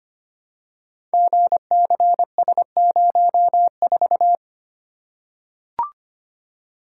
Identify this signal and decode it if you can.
Morse code, heard as a keyed tone, about 25 wpm, 700 Hz. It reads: GCS04
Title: Code Groups - 5 Letter:Numbers - 25wpm